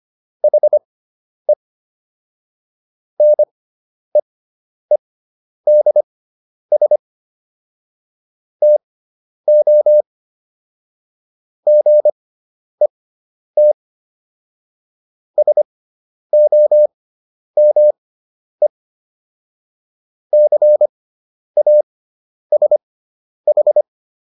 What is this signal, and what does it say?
Morse code, heard as a keyed tone, about 25 wpm, spaced as for 10 wpm, 600 Hz. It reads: HE NEEDS TO GET SOME CASH